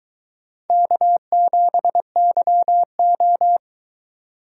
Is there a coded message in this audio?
Yes: Morse code from 0.7 to 3.6 s, heard as a keyed tone, about 22 wpm, 700 Hz. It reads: K7YO